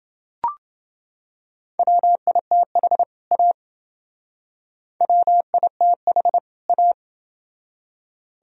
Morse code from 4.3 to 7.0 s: WITH A

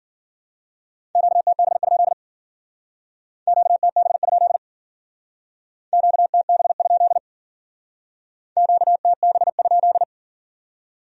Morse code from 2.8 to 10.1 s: QTB? QTB? QTB?

